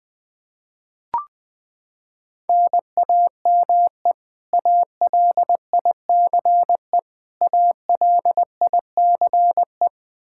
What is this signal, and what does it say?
Morse code, heard as a keyed tone, about 20 wpm, 700 Hz. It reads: NAME ALICE ALICE
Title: Practice QSO - Exchange 1 - Beginning - 20wpm